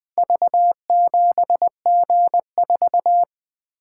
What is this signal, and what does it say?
Morse code, heard as a keyed tone, about 20 wpm, 700 Hz. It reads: V7G4
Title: Four Letter-Number Combinations - 20wpm